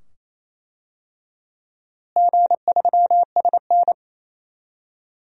Morse code, heard as a keyed tone, about 28 wpm, 700 Hz. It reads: G3SN
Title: DXCC-100 Call Signs - 4 Characters - Repeated 2x - 28wpm